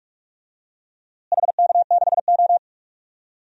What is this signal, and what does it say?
Morse code, heard as a keyed tone, about 43 wpm, 700 Hz. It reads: HK6O
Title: DXCC-100 Call Signs - 4 Characters - Repeated 2x - 45wpm